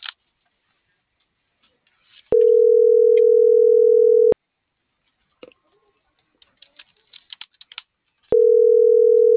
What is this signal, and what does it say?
Ringback tone (North America), 440 + 480 Hz.